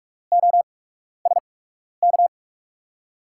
Morse code, heard as a keyed tone, about 43 wpm, spaced as for 12 wpm, 700 Hz. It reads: OSK
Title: USA Call Signs - 8x Character Spacing - 45wpm